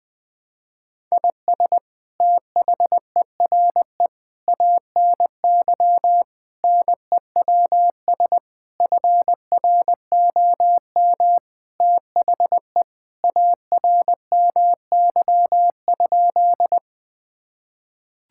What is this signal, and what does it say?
Morse code, heard as a keyed tone, about 20 wpm, 700 Hz. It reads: IS THERE ANY NEWS FROM THE ARMY?
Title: Sentences from Top 1000 Words - Review Part 5 - 20wpm